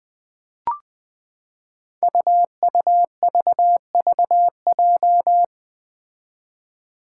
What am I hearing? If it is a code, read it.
Morse code, heard as a keyed tone, about 20 wpm, 700 Hz. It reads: UUVVJ
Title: Five Challenging Letter-Number Combinations 20wpm